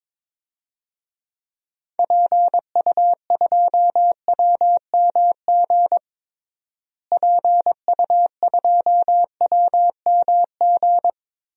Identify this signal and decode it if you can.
Morse code, heard as a keyed tone, about 22 wpm, 700 Hz. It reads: PU2WMG PU2WMG
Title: DXCC-100 Call Signs - 6 Characters - Repeated 2x - 22wpm